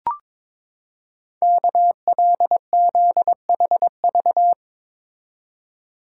Morse code, heard as a keyed tone, about 23 wpm, 700 Hz. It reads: KLZHV
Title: Code Groups - 5 Letters - 22wpm